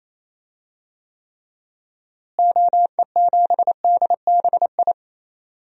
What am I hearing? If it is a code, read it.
Morse code, heard as a keyed tone, about 28 wpm, 700 Hz. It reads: OE7DBI
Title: DXCC-100 Call Signs - 6 Characters - Repeated 3x - 28wpm